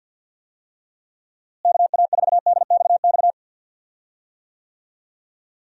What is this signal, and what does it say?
Morse code, heard as a keyed tone, about 48 wpm, 700 Hz. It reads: KA4DXX
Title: USA Call Signs - 6 Characters - Repeated 3x - 50wpm